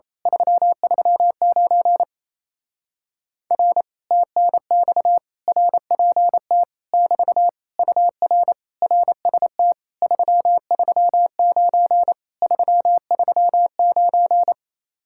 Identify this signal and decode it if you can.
Morse code, heard as a keyed tone, about 26 wpm, 700 Hz. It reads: 339 R TNX RPT <BT> UR RST 339 339
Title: Speed-Racing Common QSO Phrases Encore - 28wpm